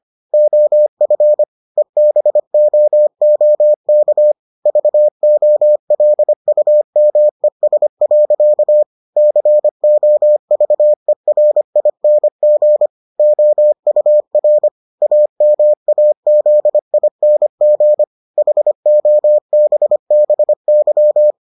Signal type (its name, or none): Morse code